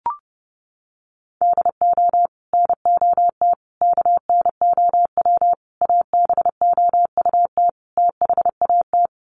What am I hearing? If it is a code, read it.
Morse code, heard as a keyed tone, about 30 wpm, 700 Hz. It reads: DO NOT KNOW ABOUT THAT
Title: Sets of 5 Words Encore2 30wpm